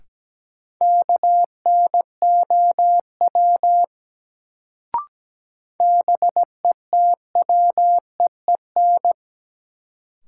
Morse code, heard as a keyed tone, about 17 wpm, 700 Hz. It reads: KNOW BETWEEN
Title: Sets of 2 Words - New Word Drill - 17wpm